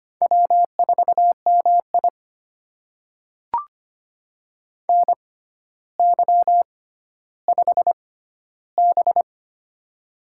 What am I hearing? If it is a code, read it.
Morse code, heard as a keyed tone, about 25 wpm, spaced as for 13 wpm, 700 Hz. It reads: W4MI NY5B
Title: USA Call Signs - 4 Characters - 6x Character Spacing - 25wpm